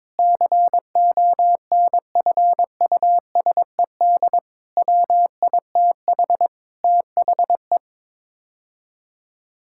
Morse code, heard as a keyed tone, about 22 wpm, 700 Hz. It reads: CONFUSED WITH THE
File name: practice / 22wpm / Sets of 6 Words - 22wpm